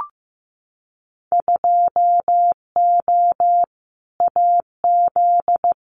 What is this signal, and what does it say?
Morse code, heard as a keyed tone, about 15 wpm, 700 Hz. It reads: 2O AZ
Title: Winter Field Day Exchanges 15wpm